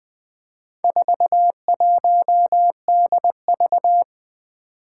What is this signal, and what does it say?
Morse code, heard as a keyed tone, about 20 wpm, 700 Hz. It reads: 41DV
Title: Four Challenging Letter-Number Combinations 20wpm